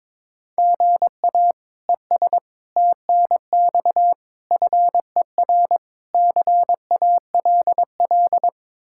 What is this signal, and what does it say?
Morse code, heard as a keyed tone, about 22 wpm, 700 Hz. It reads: GA ES TNX FER CALL
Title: Practice QSO - Exchange 1 - Beginning - 22wpm